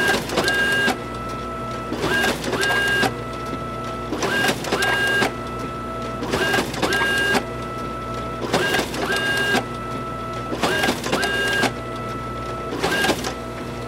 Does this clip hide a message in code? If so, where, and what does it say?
No.